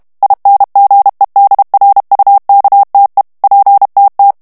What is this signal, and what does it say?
Morse code, heard as a keyed tone, about 32 wpm, 800 Hz. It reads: INGEDRUKTE PTT
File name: HOLLANDAIS texte clair 2 vitesse 30 mots minute